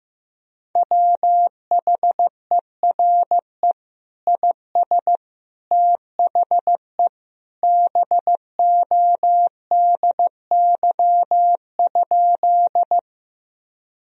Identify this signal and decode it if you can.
Morse code, heard as a keyed tone, about 15 wpm, 700 Hz. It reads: WHERE IS THE BODY?
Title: Sentences from Top 1000 Words - Review Part 2 - 15wpm